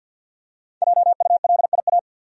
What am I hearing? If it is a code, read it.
Morse code, heard as a keyed tone, about 52 wpm, 700 Hz. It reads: JULIA
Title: USA Common Names - Max 5 Letters - 50wpm